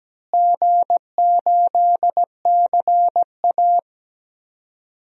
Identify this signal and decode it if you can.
Morse code, heard as a keyed tone, about 17 wpm, 700 Hz. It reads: G8CA
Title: DXCC-100 Call Signs - 4 Characters - Repeated 2x - 17wpm